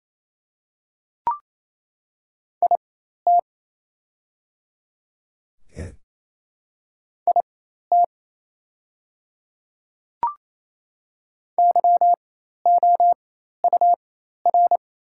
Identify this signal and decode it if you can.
Morse code, heard as a keyed tone, about 28 wpm, spaced as for 13 wpm, 700 Hz. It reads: IT IT YOUR
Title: 4x Character Spacing - Top 100 Words - 28wpm